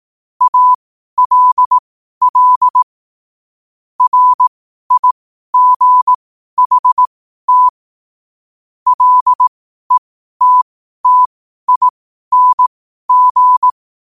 Morse code, heard as a keyed tone, about 18 wpm, spaced as for 12 wpm, 1000 Hz. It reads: ALL RIGHT LETTING
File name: qotdfiles / 12wpm